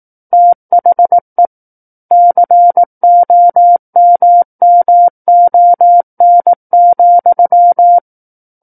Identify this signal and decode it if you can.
Morse code, heard as a keyed tone, about 18 wpm, 700 Hz. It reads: THE COMMON,
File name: 18 WPM